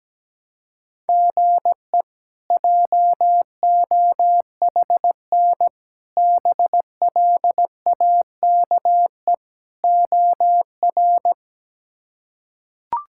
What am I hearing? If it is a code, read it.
Morse code, heard as a keyed tone, about 17 wpm, 700 Hz. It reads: GE JOHN BLAKE OR
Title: K1USN SST QSO Practice - 17wpm